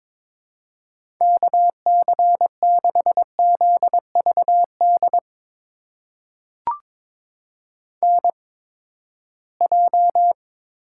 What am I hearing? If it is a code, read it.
Morse code, heard as a keyed tone, about 22 wpm, spaced as for 15 wpm, 700 Hz. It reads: KC6ZVD NJ
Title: USA Call Signs - 8x Character Spacing - 22wpm